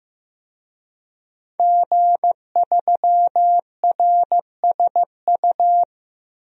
Morse code, heard as a keyed tone, about 15 wpm, 700 Hz. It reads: G3RSU
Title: DXCC-100 Call Signs - 5 Characters - Repeated 2x - 15wpm